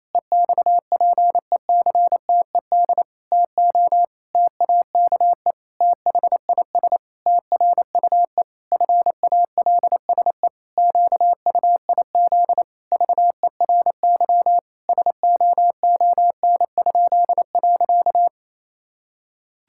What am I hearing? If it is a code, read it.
Morse code, heard as a keyed tone, about 28 wpm, 700 Hz. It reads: EXPECTED TO TAKE THIS TRUE FALSE QUIZ VERY SOON?.
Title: Pangram Challenge v2 - 28wpm